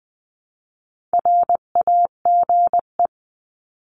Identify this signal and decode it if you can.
Morse code, heard as a keyed tone, about 20 wpm, 700 Hz. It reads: RAGE
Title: 259 Lesson 42d - Words 18-20wpm